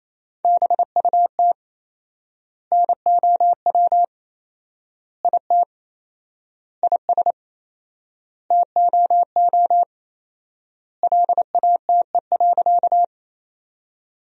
Morse code, heard as a keyed tone, about 28 wpm, 700 Hz. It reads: BUT NOW IT IS TOO LATE.
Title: 4x Word-Spacing - Sentences from Top 300 Words - 28wpm